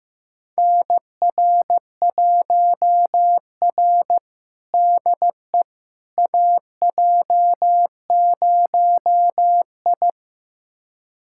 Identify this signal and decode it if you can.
Morse code, heard as a keyed tone, about 15 wpm, 700 Hz. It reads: NR1R DE AJ0I
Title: USA Callsign DE Callsign - 4 Characters - 15wpm